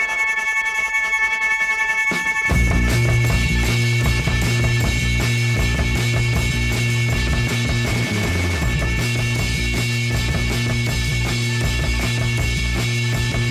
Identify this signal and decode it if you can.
Dial-up modem handshake.